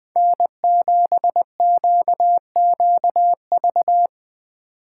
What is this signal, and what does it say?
Morse code, heard as a keyed tone, about 20 wpm, 700 Hz. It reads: N7QQV
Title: USA Call Signs - 4x Character Spacing - 20wpm